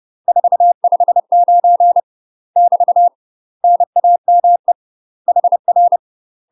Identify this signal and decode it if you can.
Morse code, heard as a keyed tone, about 30 wpm, 700 Hz. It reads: 459 <BT> NAME HR
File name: UK MASTER QSOs TRAINING / 30WPM / UK MASTER QSOs 0029